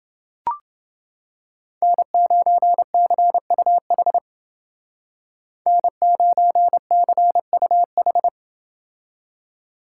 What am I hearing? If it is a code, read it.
Morse code, heard as a keyed tone, about 28 wpm, 700 Hz. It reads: N9CUH N9CUH